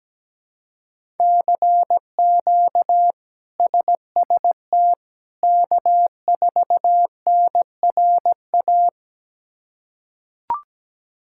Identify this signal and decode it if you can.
Morse code, heard as a keyed tone, about 17 wpm, 700 Hz. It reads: CQ SST K4NRA
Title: K1USN SST QSO Practice - 17wpm